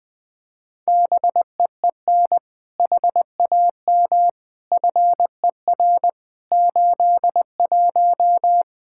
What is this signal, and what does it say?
Morse code, heard as a keyed tone, about 20 wpm, 700 Hz. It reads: BEEN HAM FER 81
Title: Practice QSO - Exchange 3 - More Info - 20wpm